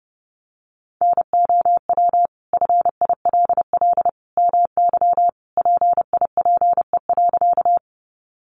Morse code, heard as a keyed tone, about 28 wpm, 700 Hz. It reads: NOW FILL MY PIPE.
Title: Sentences from Top 1500 Words 30wpm